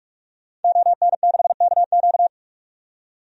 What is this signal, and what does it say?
Morse code, heard as a keyed tone, about 44 wpm, 690 Hz. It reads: ON6KQ
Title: DXCC-100 Call Signs - 5 Characters - 45wpm